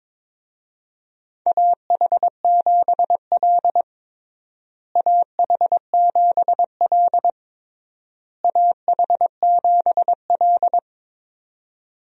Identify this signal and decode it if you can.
Morse code, heard as a keyed tone, about 22 wpm, 700 Hz. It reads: AH7L AH7L AH7L
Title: USA Call Signs - 4 Characters - Repeated 3x - 22wpm